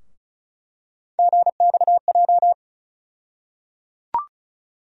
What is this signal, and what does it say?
Morse code, heard as a keyed tone, about 34 wpm, 700 Hz. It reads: GXJ
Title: USA Call Sign Postfixes - 3 Letters 35wpm